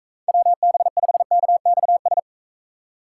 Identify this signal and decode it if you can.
Morse code, heard as a keyed tone, about 44 wpm, 700 Hz. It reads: WB5KXS